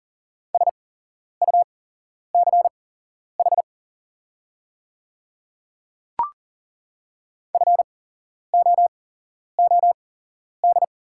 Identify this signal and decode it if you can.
Morse code, heard as a keyed tone, about 40 wpm, spaced as for 11 wpm, 700 Hz. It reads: SUCH FOOD
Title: 8x Character Spacing - Top 300 Words - 40wpm